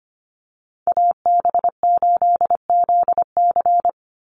Morse code, heard as a keyed tone, about 25 wpm, 700 Hz. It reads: AB8ZC
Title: USA Call Signs 25wpm